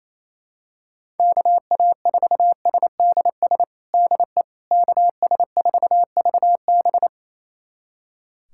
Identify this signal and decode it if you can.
Morse code, heard as a keyed tone, about 28 wpm, 700 Hz. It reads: KA4SDS DE KS4VB
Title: USA Callsign DE Callsign 28wpm